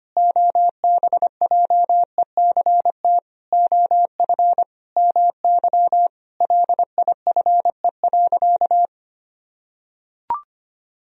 Morse code, heard as a keyed tone, about 25 wpm, 700 Hz. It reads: OBJECT OF MY LIFE.
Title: Sentences from Top 1000 Words - Review Part 3 - 25wpm